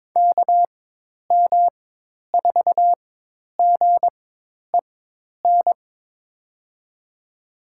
Morse code, heard as a keyed tone, about 22 wpm, spaced as for 10 wpm, 700 Hz. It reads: KM4GEN
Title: USA Call Signs - 6 Characters - 4x Character Spacing - 22wpm